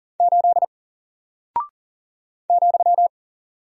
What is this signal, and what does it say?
Morse code, heard as a keyed tone, about 40 wpm, 700 Hz. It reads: 8 ,
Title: Single Character (Letter:Number:Punctuation:Prosign) 40wpm